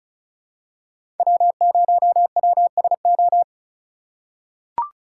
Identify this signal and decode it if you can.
Morse code, heard as a keyed tone, about 36 wpm, 700 Hz. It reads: W0WSO
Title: USA Call Signs - 5 Characters - 2x Character Spacing - 35wpm